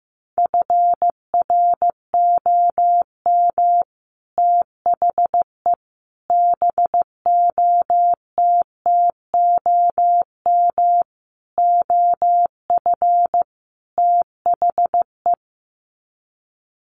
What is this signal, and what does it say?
Morse code, heard as a keyed tone, about 15 wpm, 700 Hz. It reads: FROM THE BOTTOM OF THE
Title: Sets of 5 Words Encore2 15wpm